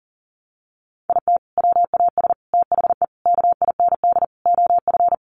Morse code, heard as a keyed tone, about 40 wpm, 700 Hz. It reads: IT WAS THE KIND OF
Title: Sets of 5 Words Encore2 40wpm